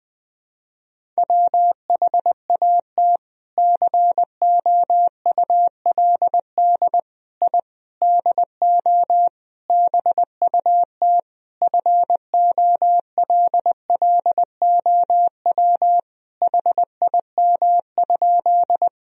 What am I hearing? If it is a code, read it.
Morse code, heard as a keyed tone, about 20 wpm, 700 Hz. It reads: WHAT COULD I DO BUT FOLLOW HIM?
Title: Sentences from Top 1000 Words - Review Part 1 - 20wpm